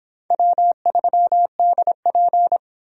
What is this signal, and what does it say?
Morse code, heard as a keyed tone, about 25 wpm, 700 Hz. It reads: W3DP